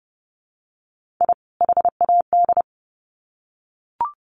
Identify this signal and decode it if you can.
Morse code, heard as a keyed tone, about 30 wpm, 700 Hz. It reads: I HAD
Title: Sets of 2 Words 30wpm